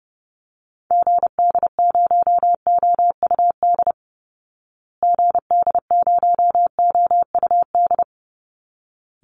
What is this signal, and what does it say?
Morse code, heard as a keyed tone, about 30 wpm, 700 Hz. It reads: GD0OUD GD0OUD
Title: Call Signs Repeated 3x - 30wpm